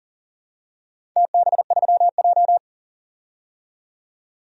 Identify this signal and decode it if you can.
Morse code, heard as a keyed tone, about 44 wpm, 700 Hz. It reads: TB3J
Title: DXCC-100 Call Signs - 4 Characters - 40wpm